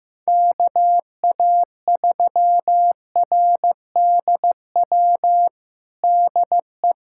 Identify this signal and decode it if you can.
Morse code, heard as a keyed tone, about 15 wpm, 700 Hz. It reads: KA3RDW DE
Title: USA Callsign DE Callsign 15wpm